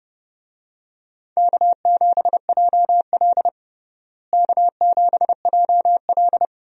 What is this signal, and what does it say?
Morse code, heard as a keyed tone, about 32 wpm, 700 Hz. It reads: K7JL K7JL
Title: USA Call Signs - 4 Characters - Repeated 2x - 30wpm